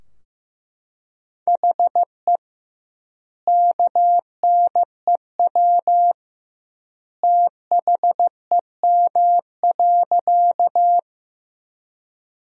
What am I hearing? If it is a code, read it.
Morse code, heard as a keyed tone, about 15 wpm, 700 Hz. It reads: HE KNEW THEM.
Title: 2x Word-Spacing - Sentences from Top 400 Words - 15wpm